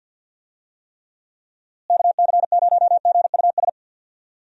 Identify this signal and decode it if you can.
Morse code, heard as a keyed tone, about 46 wpm, 700 Hz. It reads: KC0GUS
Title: USA Call Signs - 6 Characters - Repeated 2x - 50wpm